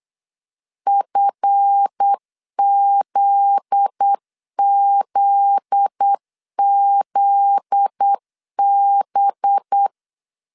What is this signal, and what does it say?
Morse code, heard as a keyed tone, about 8 wpm, 800 Hz. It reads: FZZZB
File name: LEZIONI / 7 GF9Z / 7 GF9Z 8wpm 3213